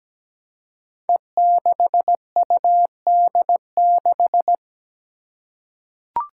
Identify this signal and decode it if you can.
Morse code, heard as a keyed tone, about 17 wpm, 700 Hz. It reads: E6UD6
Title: Five Challenging Letter-Number Combinations 17wpm